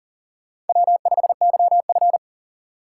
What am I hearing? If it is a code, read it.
Morse code, heard as a keyed tone, about 51 wpm, 700 Hz. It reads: W5YF